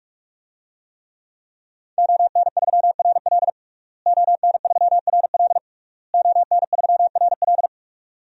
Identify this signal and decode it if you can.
Morse code, heard as a keyed tone, about 46 wpm, 700 Hz. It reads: ON3RL ON3RL ON3RL
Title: DXCC-100 Call Signs - 5 Characters - Repeated 3x - 45wpm